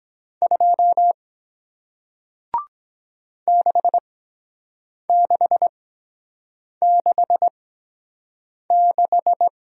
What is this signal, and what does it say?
Morse code, heard as a keyed tone, about 23 wpm, spaced as for 7 wpm, 700 Hz. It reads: 2 6666